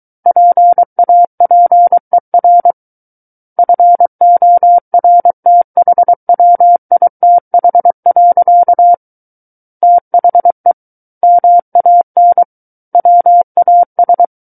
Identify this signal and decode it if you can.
Morse code, heard as a keyed tone, about 23 wpm, 700 Hz. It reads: PAPER FORTHWITH. THE MAN WAS
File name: Book One / 14-Chapter 14 - 23 WPM